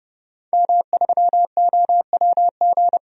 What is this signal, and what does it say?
Morse code, heard as a keyed tone, about 30 wpm, 700 Hz. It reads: M3OWG
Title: DXCC-100 Call Signs - 5 Characters - 30wpm